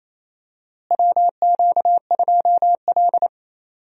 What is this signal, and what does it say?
Morse code, heard as a keyed tone, about 27 wpm, 700 Hz. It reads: WQ2L
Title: USA Call Signs - 4 Characters - Repeated 2x - 28wpm